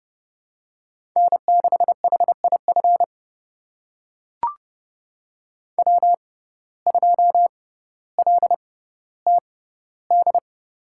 Morse code, heard as a keyed tone, about 30 wpm, spaced as for 15 wpm, 700 Hz. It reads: N6HIF W2LTD
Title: USA Call Signs - 5 Characters - 6x Character Spacing - 30wpm